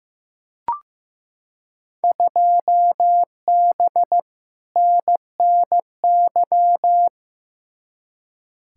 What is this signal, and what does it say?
Morse code, heard as a keyed tone, about 15 wpm, 700 Hz. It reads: 2B NNY
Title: Field Day Exchanges Encore - 15wpm